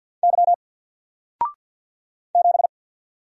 Morse code, heard as a keyed tone, about 48 wpm, 700 Hz. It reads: Y 7
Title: Single Letter-Number 50wpm